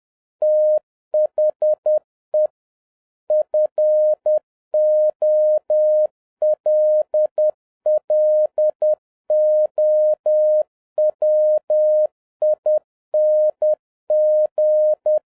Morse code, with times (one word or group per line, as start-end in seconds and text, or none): 0.4-2.5 THE
3.3-15.2 FOLLOWING